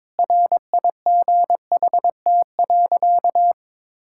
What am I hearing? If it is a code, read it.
Morse code, heard as a keyed tone, about 22 wpm, 700 Hz. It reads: RIGHT.
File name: practice / 22wpm / Sentences with 4-Word Phrases - 22wpm